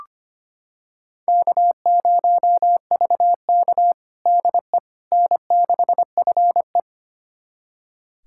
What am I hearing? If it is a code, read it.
Morse code, heard as a keyed tone, about 25 wpm, 700 Hz. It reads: K0VK DE N6FE
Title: USA Callsign DE Callsign - 4 Characters - 25wpm